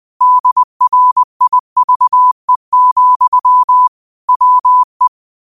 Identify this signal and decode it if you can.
Morse code, heard as a keyed tone, about 20 wpm, 1000 Hz. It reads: DRIVE, WE